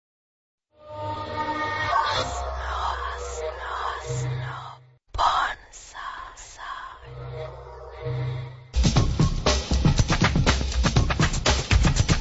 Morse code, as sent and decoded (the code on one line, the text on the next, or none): none